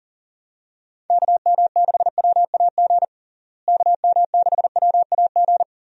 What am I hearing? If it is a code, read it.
Morse code, heard as a keyed tone, about 37 wpm, 700 Hz. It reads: KM6WAG KM6WAG
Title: USA Call Signs - 6 Characters - Repeated 3x - 40wpm